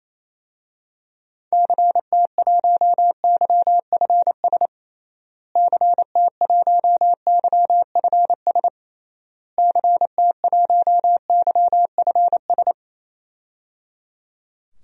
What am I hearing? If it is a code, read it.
Morse code, heard as a keyed tone, about 28 wpm, 700 Hz. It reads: CT1YFS CT1YFS CT1YFS
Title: DXCC-100 Call Signs - 6 Characters - Repeated 3x - 28wpm